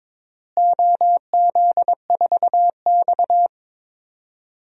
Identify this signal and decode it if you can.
Morse code, heard as a keyed tone, about 22 wpm, 700 Hz. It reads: OZ4X
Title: DXCC-100 Call Signs - 4 Characters - Repeated 3x - 22wpm